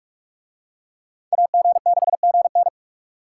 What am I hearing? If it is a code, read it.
Morse code, heard as a keyed tone, about 44 wpm, 700 Hz. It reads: AG6GN